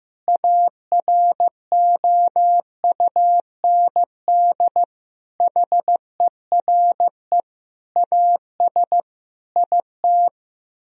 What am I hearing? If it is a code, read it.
Morse code, heard as a keyed tone, about 15 wpm, 700 Hz. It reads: AROUND HERE AS IT
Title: Sentences from Top 1000 Words - Review Part 3 - 15wpm